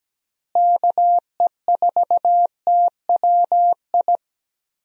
Morse code, heard as a keyed tone, about 17 wpm, 700 Hz. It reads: KE4TWI
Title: USA Call Signs - 6 Characters - Repeated 3x - 17wpm